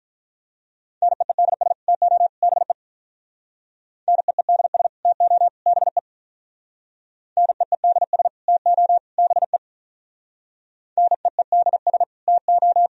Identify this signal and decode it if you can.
Morse code, heard as a keyed tone, about 43 wpm, 700 Hz. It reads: NEEDS TO BE NEEDS TO BE NEEDS TO BE NEEDS TO
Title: Speed-Racing - Sets of 3 Words 35wpm